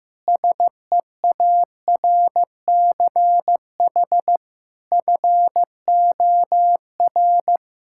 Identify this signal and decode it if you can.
Morse code, heard as a keyed tone, about 15 wpm, 700 Hz. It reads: SEARCH FOR
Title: Sentences from Top 1000 Words - Review Part 5 - 15wpm